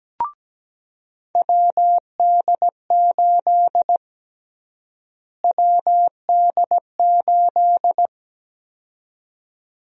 Morse code, heard as a keyed tone, about 17 wpm, 700 Hz. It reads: WD8 WD8
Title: USA Call Sign Prefixes - Repeated 2x - 17wpm